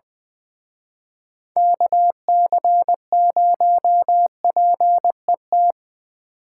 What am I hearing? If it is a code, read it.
Morse code, heard as a keyed tone, about 22 wpm, 700 Hz. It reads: KC0PET
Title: Speed Racing - USA 2x3 Call Signs 20wpm